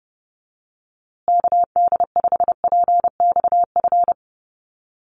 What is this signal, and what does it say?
Morse code, heard as a keyed tone, about 28 wpm, 700 Hz. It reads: KD5PXF
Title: USA Call Signs Repeated 3x 30wpm